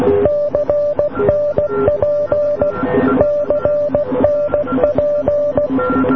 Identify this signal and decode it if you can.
Morse code, heard as a keyed tone, about 16 wpm, 590 Hz, with noise. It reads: CNP CNP